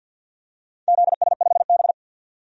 Morse code, heard as a keyed tone, about 47 wpm, 700 Hz. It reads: ZS5B
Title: DXCC-100 Call Signs - 4 Characters - Repeated 2x - 50wpm